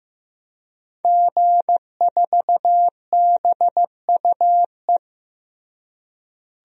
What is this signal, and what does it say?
Morse code, heard as a keyed tone, about 15 wpm, 700 Hz. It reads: G4BUE
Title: DXCC-100 Call Signs - 5 Characters - Repeated 3x - 15wpm